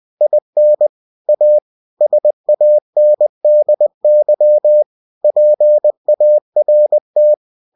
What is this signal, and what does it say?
Morse code, heard as a keyed tone, about 20 wpm, 600 Hz. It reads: IN A SANDY PART
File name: cw-books / winnie-the-pooh / winnie-20wpm-page-0066